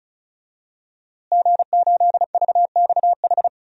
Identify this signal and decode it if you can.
Morse code, heard as a keyed tone, about 36 wpm, 700 Hz. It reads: G8VXH
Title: DXCC-100 Call Signs - 5 Characters - Repeated 2x - 35wpm